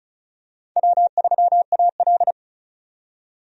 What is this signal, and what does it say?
Morse code, heard as a keyed tone, about 36 wpm, 700 Hz. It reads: W3AL
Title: USA Call Signs - 4 Characters - 35wpm